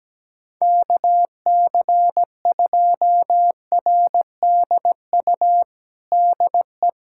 Morse code, heard as a keyed tone, about 17 wpm, 700 Hz. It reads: KC2RDU DE
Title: USA Callsign DE Callsign 17wpm